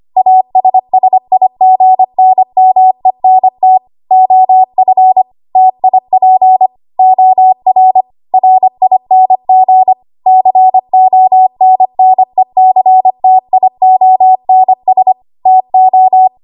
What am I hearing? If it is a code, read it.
Morse code, heard as a keyed tone, about 25 wpm, 750 Hz. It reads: ASSIGNMENT OF TIP OR RING CONNECTIONS TO
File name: Archive / 25 WPM / 25WPM